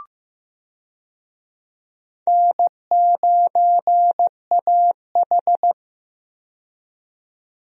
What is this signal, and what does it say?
Morse code, heard as a keyed tone, about 15 wpm, 700 Hz. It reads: N9AH
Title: USA Call Signs - 4 Characters - 15wpm